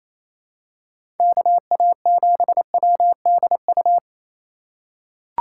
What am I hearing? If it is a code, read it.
Morse code, heard as a keyed tone, about 28 wpm, 700 Hz. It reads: KA7WDU
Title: USA Call Signs - 6 Characters - 2x Character Spacing - 28wpm